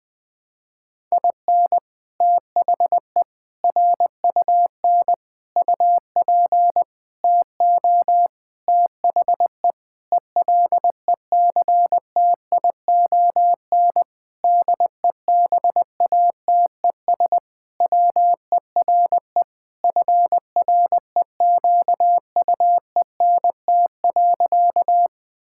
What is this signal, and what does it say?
Morse code, heard as a keyed tone, about 20 wpm, 700 Hz. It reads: IN THE RUN UP TO THE ELECTION DEBATES WERE FREQUENT.
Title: Sentences with 6-Word Phrases - Spoken Beginning - 20wpm